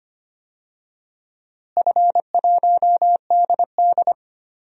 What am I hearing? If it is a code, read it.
Morse code, heard as a keyed tone, about 25 wpm, 700 Hz. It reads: F1DD
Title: DXCC-100 Call Signs - 4 Characters - Repeated 2x - 25wpm